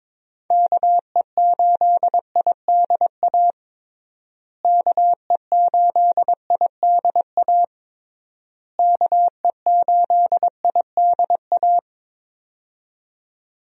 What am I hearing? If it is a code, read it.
Morse code, heard as a keyed tone, about 22 wpm, 700 Hz. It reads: KE8IDA KE8IDA KE8IDA
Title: USA Call Signs - 6 Characters - Repeated 3x - 22wpm